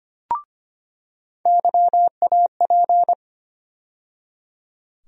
Morse code, heard as a keyed tone, about 25 wpm, 700 Hz. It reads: YAP
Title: USA Call Sign Postfixes - 3 Letters 25wpm